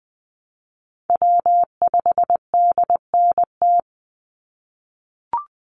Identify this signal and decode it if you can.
Morse code, heard as a keyed tone, about 19 wpm, 700 Hz. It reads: W5DNT
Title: Call Signs - 20wpm